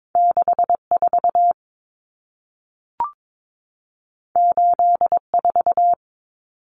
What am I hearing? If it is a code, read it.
Morse code, heard as a keyed tone, about 22 wpm, 700 Hz. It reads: 64 84
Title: Numbers - 2-Digit 22wpm